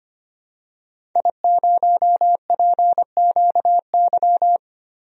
Morse code, heard as a keyed tone, about 25 wpm, 700 Hz. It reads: I0PQY
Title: DXCC-100 Call Signs - 5 Characters - 25wpm